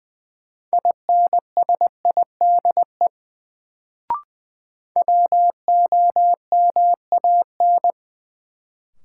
Morse code, heard as a keyed tone, about 20 wpm, 700 Hz. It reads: INSIDE WOMAN
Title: Sentences from Top 1000 Words - Review Part 2 - New Word Drill - 20wpm